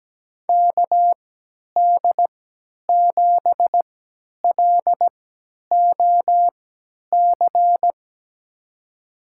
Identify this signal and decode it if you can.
Morse code, heard as a keyed tone, about 17 wpm, spaced as for 10 wpm, 700 Hz. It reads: KD7LOC
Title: USA Call Signs - 6 Characters - 3x Character Spacing - 17wpm